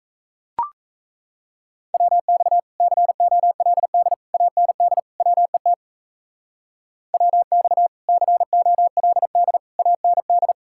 Morse code, heard as a keyed tone, about 33 wpm, 700 Hz. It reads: WX COLD AND WET WX COLD AND
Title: Speed-Racing Common QSO Phrases - 28wpm